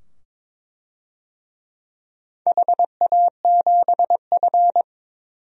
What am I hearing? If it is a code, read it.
Morse code, heard as a keyed tone, about 23 wpm, 690 Hz. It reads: HA7F
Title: DXCC-100 Call Signs - 4 Characters - 22wpm